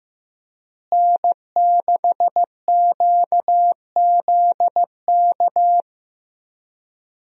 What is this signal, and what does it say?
Morse code, heard as a keyed tone, about 16 wpm, 700 Hz. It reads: N6QZK
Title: USA Call Signs - 5 Characters - Repeated 3x - 15wpm